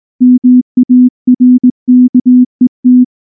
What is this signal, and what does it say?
Morse code, heard as a keyed tone, about 19 wpm, 250 Hz. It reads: MARKET
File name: morse / morse-training-20WPM-3